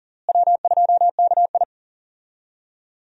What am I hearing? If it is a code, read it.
Morse code, heard as a keyed tone, about 37 wpm, 700 Hz. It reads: W2KI
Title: USA Call Signs - 4 Characters - Repeated 2x - 40wpm